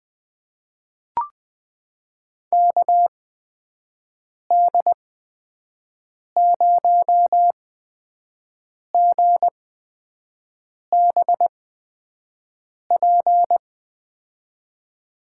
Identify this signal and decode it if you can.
Morse code, heard as a keyed tone, about 20 wpm, spaced as for 5 wpm, 700 Hz. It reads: KD0GBP
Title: USA Call Signs - 8x Character Spacing - 20wpm